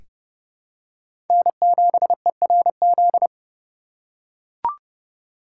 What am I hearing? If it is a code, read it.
Morse code, heard as a keyed tone, about 32 wpm, 700 Hz. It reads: N7ERZ
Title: USA Call Signs - 5 Characters - 3x Character Spacing - 30wpm